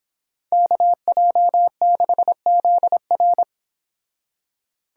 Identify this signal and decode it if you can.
Morse code, heard as a keyed tone, about 26 wpm, 700 Hz. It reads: KJ6ZR
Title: Speed Racing - USA Call Signs 22wpm